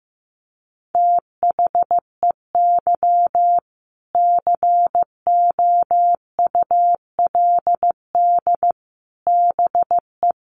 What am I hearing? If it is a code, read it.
Morse code, heard as a keyed tone, about 15 wpm, 700 Hz. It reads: THEY COULD BE
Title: Sentences from Top 1200 Words 15wpm